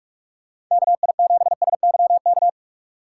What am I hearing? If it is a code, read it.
Morse code, heard as a keyed tone, about 45 wpm, 700 Hz. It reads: KI7SYK